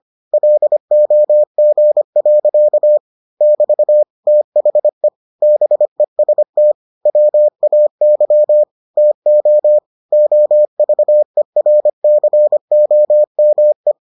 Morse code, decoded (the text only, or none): LOG. <BT> THE BEST WAY TO OVERCOME